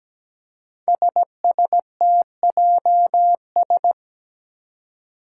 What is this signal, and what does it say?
Morse code, heard as a keyed tone, about 17 wpm, 700 Hz. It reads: SSTJS
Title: Five Challenging Letter-Number Combinations 17wpm